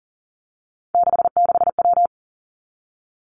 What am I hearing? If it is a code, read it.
Morse code, heard as a keyed tone, about 38 wpm, 700 Hz. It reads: 66W
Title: Three Challenging Letter-Number Combinations 40wpm